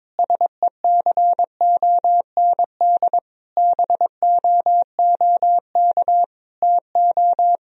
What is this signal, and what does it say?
Morse code, heard as a keyed tone, about 22 wpm, 700 Hz. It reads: SECOND BOOK TO
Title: Sentences from Top 1000 Words - Review Part 1 - 22wpm